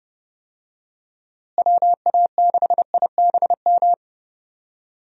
Morse code, heard as a keyed tone, about 30 wpm, 700 Hz. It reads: WA6IBM
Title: USA Call Signs - 6 Characters - Repeated 2x - 30wpm